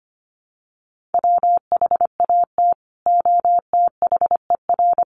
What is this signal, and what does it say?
Morse code, heard as a keyed tone, about 25 wpm, 700 Hz. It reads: WHAT OTHER
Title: Sentences from Top 1400 Words 25wpm